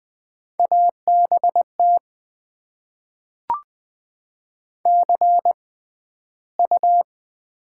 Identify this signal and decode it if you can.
Morse code, heard as a keyed tone, about 20 wpm, spaced as for 12 wpm, 700 Hz. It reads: ABT CU
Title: Top 90 QSO Elements - 6x Character Spacing - 20wpm